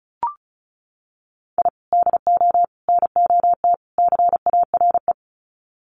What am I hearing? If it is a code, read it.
Morse code, heard as a keyed tone, about 34 wpm, 700 Hz. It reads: I DO NOT CARE
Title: Sets of 4 Words 35wpm